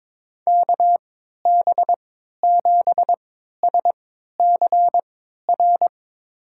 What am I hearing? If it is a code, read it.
Morse code, heard as a keyed tone, about 22 wpm, spaced as for 13 wpm, 700 Hz. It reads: KB7SCR
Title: USA Call Signs - 3x Character Spacing - 22wpm